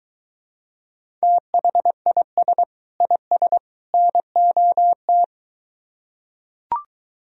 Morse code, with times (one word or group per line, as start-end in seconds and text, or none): 1.2-2.6 THIS
3.0-3.6 IS
3.9-5.2 NOT